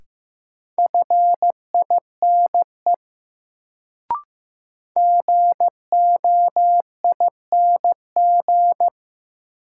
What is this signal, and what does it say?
Morse code, heard as a keyed tone, about 15 wpm, 700 Hz. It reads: FINE GOING
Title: Sentences from Top 500 Words - New Word Drill - 15wpm